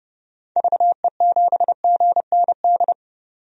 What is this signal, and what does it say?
Morse code, heard as a keyed tone, about 31 wpm, 700 Hz. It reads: VE7GND
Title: DXCC-100 Call Signs - 6 Characters - 30wpm